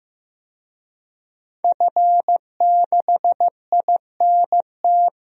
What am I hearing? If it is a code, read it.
Morse code, heard as a keyed tone, about 15 wpm, 700 Hz. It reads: F6INT
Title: DXCC-100 Call Signs - 5 Characters - 15wpm